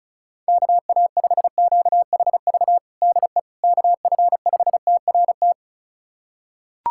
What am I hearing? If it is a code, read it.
Morse code, heard as a keyed tone, about 36 wpm, 700 Hz. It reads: KA5QHV DE KF5TRT
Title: USA Callsign DE Callsign - 6 Characters - 35wpm